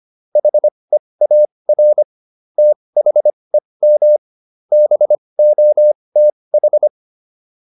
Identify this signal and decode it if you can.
Morse code, heard as a keyed tone, about 25 wpm, spaced as for 20 wpm, 600 Hz. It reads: HEAR THEM BOTH
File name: PR302 20